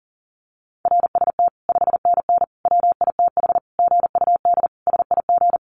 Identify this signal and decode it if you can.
Morse code, heard as a keyed tone, about 40 wpm, 700 Hz. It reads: RST 5NN WITH GUD SIG
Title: Common QSO Phrases 40wpm